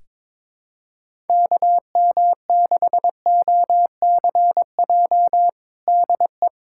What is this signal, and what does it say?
Morse code, heard as a keyed tone, about 22 wpm, 700 Hz. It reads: KM6OCJ DE
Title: USA Callsign DE Callsign - 6 Characters - 22wpm